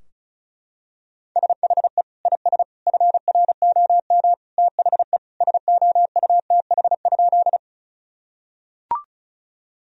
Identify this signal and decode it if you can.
Morse code, heard as a keyed tone, about 36 wpm, 700 Hz. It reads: SHE IS FROM THE SOUTH?
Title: Sentences from Top 1000 Words - Review Part 2 - 35wpm